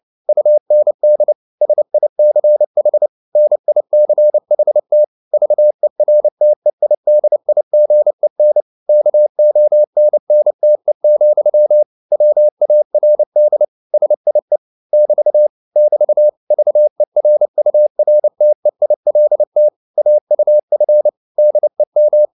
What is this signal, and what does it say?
Morse code, heard as a keyed tone, about 29 wpm, 600 Hz. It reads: UND SICH NICHT VERTEIDIGEN KONNTE, WARD SIE <BT> <BT> VERURTEILT AUF DEM